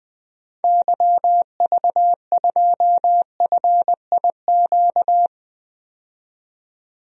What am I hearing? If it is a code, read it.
Morse code, heard as a keyed tone, about 20 wpm, 700 Hz. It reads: YV2FIQ
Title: DXCC-100 Call Signs - 6 Characters - 20wpm